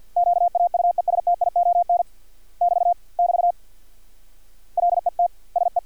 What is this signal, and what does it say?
Morse code, heard as a keyed tone, about 50 wpm, 700 Hz. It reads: QAUESTION <BT> <BT> LET SE
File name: archived / CRSnet